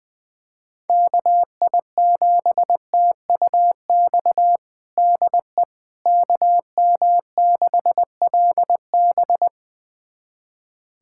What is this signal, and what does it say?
Morse code, heard as a keyed tone, about 20 wpm, 700 Hz. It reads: KI7TUX DE KM6LB
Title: USA Callsign DE Callsign 20wpm